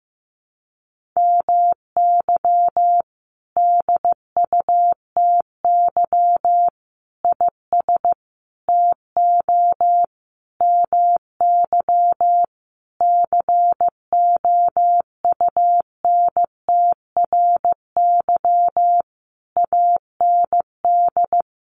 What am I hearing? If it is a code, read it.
Morse code, heard as a keyed tone, about 15 wpm, 700 Hz. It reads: MY DUTY IS TO MY COUNTRY AND
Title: Sentences from Top 1500 Words 15wpm